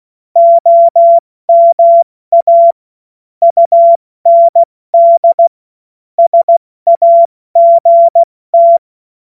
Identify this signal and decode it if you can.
Morse code, heard as a keyed tone, about 16 wpm, 680 Hz. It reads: OMA UND SAGT
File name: Morsekurs / Witze / Witz 28. September 2021, 70 F80